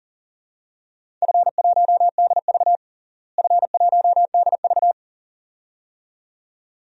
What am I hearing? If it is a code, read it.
Morse code, heard as a keyed tone, about 40 wpm, 700 Hz. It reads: F1DV F1DV
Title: DXCC-100 Call Signs - 4 Characters - Repeated 2x - 40wpm